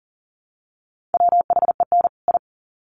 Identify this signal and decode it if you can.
Morse code, heard as a keyed tone, about 40 wpm, 700 Hz. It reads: WHEN I
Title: Sets of 2 Words 40wpm